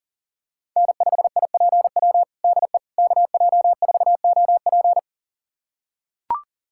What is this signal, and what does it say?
Morse code, heard as a keyed tone, about 40 wpm, 700 Hz. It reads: N5IPW DE KJ4OP
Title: USA Callsign DE Callsign - 5 Characters - 40wpm